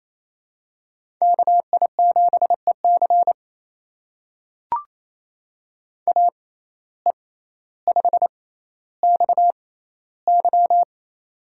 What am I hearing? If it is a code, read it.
Morse code, heard as a keyed tone, about 28 wpm, spaced as for 14 wpm, 700 Hz. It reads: KI7EC AE5XY
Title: USA Call Signs - 5 Characters - 6x Character Spacing - 28wpm